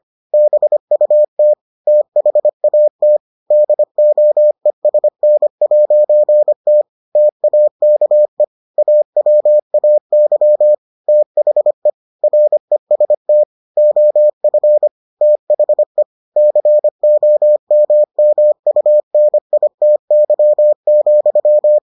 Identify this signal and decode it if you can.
Morse code, heard as a keyed tone, about 25 wpm, 600 Hz. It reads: BUT THAT DOESN'T TAKE AWAY THE REST OF THE COMMUNITY,